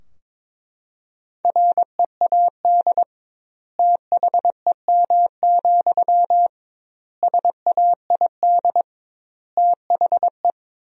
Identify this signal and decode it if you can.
Morse code, heard as a keyed tone, about 22 wpm, 700 Hz. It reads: READ THEM, SAID THE
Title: 2x Word-Spacing - Sentences from Top 400 Words - 22wpm